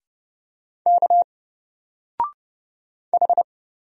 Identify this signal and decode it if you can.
Morse code, heard as a keyed tone, about 30 wpm, 700 Hz. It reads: K H
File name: Single-Letter-Number-30wpm